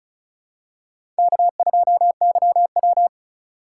Morse code, heard as a keyed tone, about 36 wpm, 700 Hz. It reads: K2YW
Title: USA Call Signs - 4 Characters - 4x Character Spacing - 35wpm